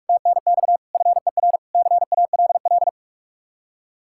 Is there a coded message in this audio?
Yes: Morse code from 0.1 to 2.9 s, heard as a keyed tone, about 48 wpm, 700 Hz. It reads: TNX FER CALL